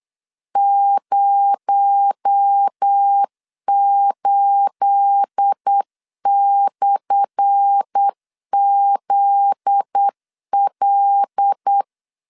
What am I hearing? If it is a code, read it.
Morse code, heard as a keyed tone, about 8 wpm, 800 Hz. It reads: 08/ZL
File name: LEZIONI / 10 SIGNS / 10A SLASH 8wpm 3452